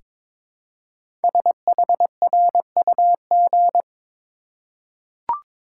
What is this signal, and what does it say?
Morse code, heard as a keyed tone, about 22 wpm, 700 Hz. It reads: SHRUG
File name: 5 Letter Words - First 3 Letters Given - 22wpm